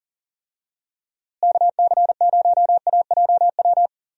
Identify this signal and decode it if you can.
Morse code, heard as a keyed tone, about 40 wpm, 700 Hz. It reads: KC0AJW
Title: USA Call Signs - 6 Characters - Repeated 3x - 40wpm